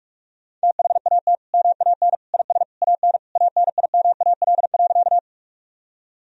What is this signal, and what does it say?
Morse code, heard as a keyed tone, about 46 wpm, 700 Hz. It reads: THAT MAN IS AN ANIMAL.
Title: Sentences from Top 1000 Words - Review Part 1 - 45wpm